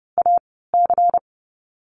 Morse code, heard as a keyed tone, about 30 wpm, 700 Hz. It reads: A C
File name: Lesson 37g - Letter x - Call Signs 30wpm